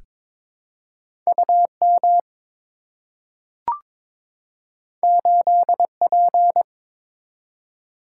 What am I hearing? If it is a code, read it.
Morse code, heard as a keyed tone, about 22 wpm, 700 Hz. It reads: UM 8P
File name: practice / 22wpm / Two Letter-Number Combinations - 22wpm